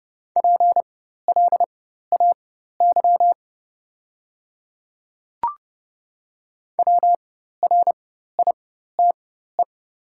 Morse code, heard as a keyed tone, about 30 wpm, spaced as for 14 wpm, 700 Hz. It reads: PLAY WRITE
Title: 4x Character Spacing - Top 300 Words - 30wpm